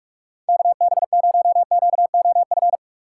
Morse code, heard as a keyed tone, about 46 wpm, 690 Hz. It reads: KB0QOF
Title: USA Call Signs - 6 Characters - 3x Character Spacing - 45wpm